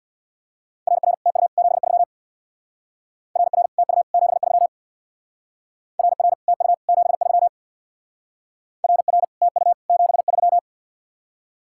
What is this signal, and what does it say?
Morse code, heard as a keyed tone, about 61 wpm, 700 Hz. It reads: RR TU 73 RR TU 73 RR TU 73 RR TU 73
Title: Speed-Racing Common QSO Phrases - 50wpm